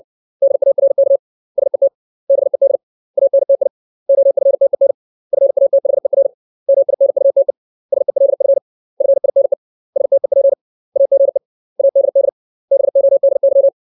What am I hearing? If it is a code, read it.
Morse code, heard as a keyed tone, about 60 wpm, 550 Hz. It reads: DARK SET BED ANTI OFTEN FATHER MINUTE SELF FINE STEP AGE ADD BODY
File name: 632 words / 632 Words 60WPM